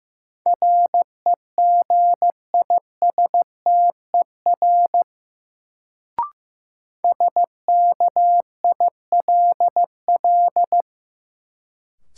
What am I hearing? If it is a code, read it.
Morse code, heard as a keyed tone, about 15 wpm, 700 Hz. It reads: REGISTER SKILL
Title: Sentences from Top 1200 Words - New Word Drill Alt - 15wpm